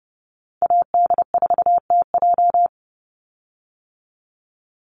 Morse code, heard as a keyed tone, about 30 wpm, 700 Hz. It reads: AD4TJ
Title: Call Signs - 30wpm